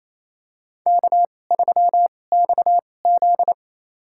Morse code, heard as a keyed tone, about 28 wpm, spaced as for 20 wpm, 700 Hz. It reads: K3XZ
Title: USA Call Signs - 4 Characters - 2x Character Spacing - 28wpm